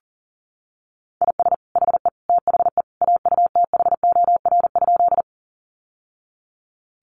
Morse code, heard as a keyed tone, about 37 wpm, 700 Hz. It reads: IS HE THE AUTHOR?
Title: Sentences from Top 1200 Words 40wpm